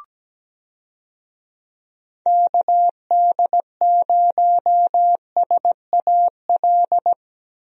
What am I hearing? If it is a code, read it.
Morse code, heard as a keyed tone, about 17 wpm, 700 Hz. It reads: KD0SAL
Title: USA Call Signs - 6 Characters - 17wpm